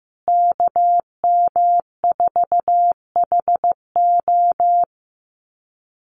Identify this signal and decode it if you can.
Morse code, heard as a keyed tone, about 15 wpm, 700 Hz. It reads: KM4HO
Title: USA Call Signs Repeated 3x 15wpm